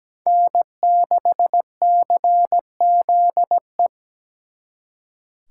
Morse code, heard as a keyed tone, about 20 wpm, 700 Hz. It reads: N6CZE
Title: Speed Racing - USA 1x3 Call Signs 17wpm